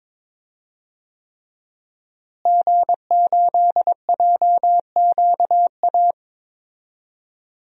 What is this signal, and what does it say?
Morse code, heard as a keyed tone, about 22 wpm, 700 Hz. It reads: G8JQA
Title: DXCC-100 Call Signs - 5 Characters - Repeated 3x - 22wpm